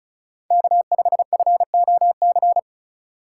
Morse code, heard as a keyed tone, about 36 wpm, 700 Hz. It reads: K5FOC
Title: USA Call Signs - 5 Characters - 3x Character Spacing - 35wpm